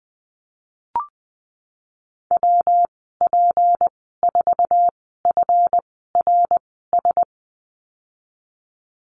Morse code, heard as a keyed tone, about 20 wpm, spaced as for 14 wpm, 700 Hz. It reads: WP4FRS
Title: 154 Lesson 25g - Letter p - Call Signs 20wpm